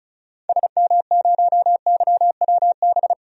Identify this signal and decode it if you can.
Morse code, heard as a keyed tone, about 34 wpm, 700 Hz. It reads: SM0YWB
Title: DXCC-100 Call Signs - 6 Characters - 35wpm